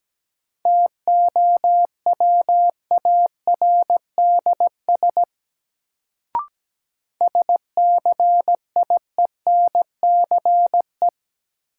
Morse code, heard as a keyed tone, about 17 wpm, 700 Hz. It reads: TOWARDS SCIENCE
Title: Sentences from Top 1000 Words - Review Part 2 - New Word Drill - 17wpm